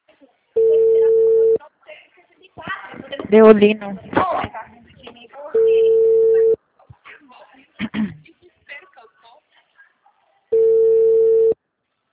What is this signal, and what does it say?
Ringback tone (China, 450 Hz), with speech.
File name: out- -2011- - - .12